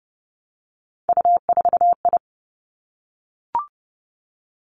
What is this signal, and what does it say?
Morse code, heard as a keyed tone, about 30 wpm, 700 Hz. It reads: U4I
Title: Three Challenging Letter-Number Combinations 30wpm